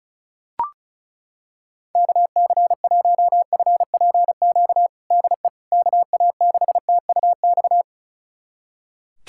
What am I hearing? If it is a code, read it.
Morse code, heard as a keyed tone, about 34 wpm, 700 Hz. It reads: KC1FPQ DE KA6TUX
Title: USA Callsign DE Callsign 35wpm